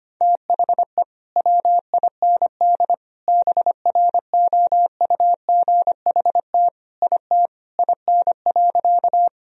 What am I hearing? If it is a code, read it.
Morse code, heard as a keyed tone, about 25 wpm, 700 Hz. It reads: THE WIND BROUGHT IT IN.
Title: Sentences from Top 1000 Words - Review Part 2 - 25wpm